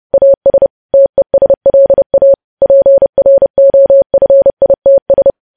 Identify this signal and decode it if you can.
Morse code, heard as a keyed tone, about 30 wpm, 550 Hz. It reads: AS TESLA PROFITS
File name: podcast / News headlines-1x-30x30